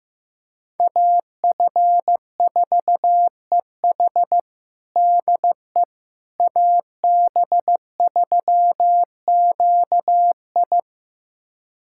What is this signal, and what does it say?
Morse code, heard as a keyed tone, about 15 wpm, 700 Hz. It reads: AF4EH DE AB3QI
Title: USA Callsign DE Callsign - 5 Characters - 15wpm